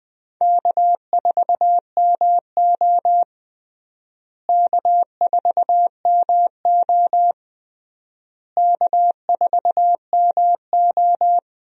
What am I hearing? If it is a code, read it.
Morse code, heard as a keyed tone, about 19 wpm, 700 Hz. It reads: K4MO K4MO K4MO
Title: USA Call Signs - 4 Characters - Repeated 3x - 20wpm